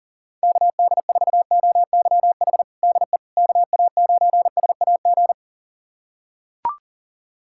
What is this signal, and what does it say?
Morse code, heard as a keyed tone, about 40 wpm, 700 Hz. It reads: KD4OYH DE KA9SAG
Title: USA Callsign DE Callsign - 6 Characters - 40wpm